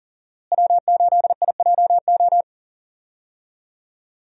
Morse code, heard as a keyed tone, about 40 wpm, 700 Hz. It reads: W8IJO
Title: USA Call Signs - 5 Characters - Repeated 3x - 40wpm